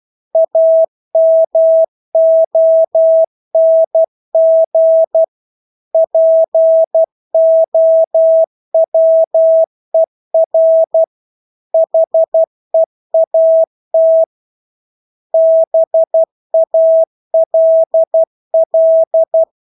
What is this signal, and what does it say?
Morse code, heard as a keyed tone, about 12 wpm, 650 Hz. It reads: AMONG POWER HEAT BALL
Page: Engelske ord 12 wpm